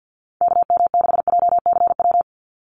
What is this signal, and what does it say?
Morse code, heard as a keyed tone, about 49 wpm, 700 Hz. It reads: KM6JCW
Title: USA Call Signs 50wpm